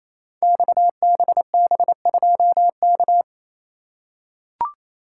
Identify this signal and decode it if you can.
Morse code, heard as a keyed tone, about 27 wpm, 700 Hz. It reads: XBB2K
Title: Code Groups - 5 Letter:Numbers - 28wpm